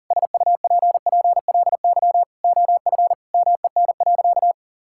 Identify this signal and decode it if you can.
Morse code, heard as a keyed tone, about 40 wpm, 700 Hz. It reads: SUPPLY OF MEN.
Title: Sentences from Top 1000 Words - Review Part 5 - 40wpm